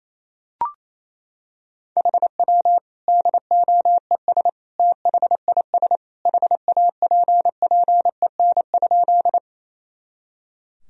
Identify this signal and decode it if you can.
Morse code, heard as a keyed tone, about 28 wpm, 700 Hz. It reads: HW DOES THIS HAPPEN?
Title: Sentences from Top 400 Words and Abbreviations 28wpm